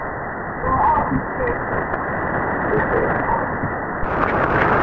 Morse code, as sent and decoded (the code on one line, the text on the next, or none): none